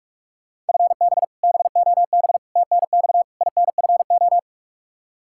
Morse code, heard as a keyed tone, about 43 wpm, 700 Hz. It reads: FB BOB TNX INFO
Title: K1USN SST QSO Practice 45wpm